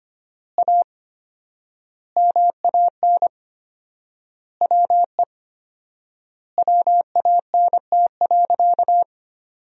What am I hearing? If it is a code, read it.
Morse code, heard as a keyed tone, about 25 wpm, 700 Hz. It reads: A MAN WE WANT.
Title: 4x Word-Spacing - Sentences from Top 200 Words - 25wpm